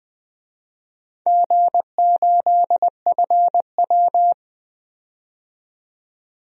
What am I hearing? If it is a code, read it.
Morse code, heard as a keyed tone, about 20 wpm, 700 Hz. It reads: G8FW
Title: DXCC-100 Call Signs - 4 Characters - 20wpm